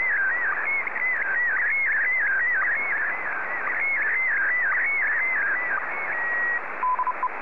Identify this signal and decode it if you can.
SSTV (slow-scan television), Scottie DX.